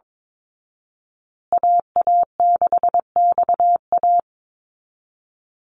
Morse code, heard as a keyed tone, about 22 wpm, 700 Hz. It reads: AA6XA
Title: Call Signs Repeated 3x - 22wpm